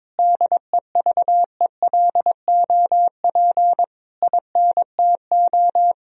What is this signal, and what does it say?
Morse code, heard as a keyed tone, about 22 wpm, 700 Hz. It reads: DEVELOP INTO
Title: Sentences from Top 1000 Words - Review Part 2 - 22wpm